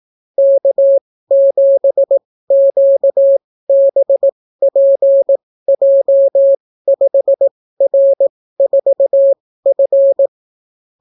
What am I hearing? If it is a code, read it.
Morse code, heard as a keyed tone, about 18 wpm, spaced as for 14 wpm, 550 Hz. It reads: K7QBPJ5R4F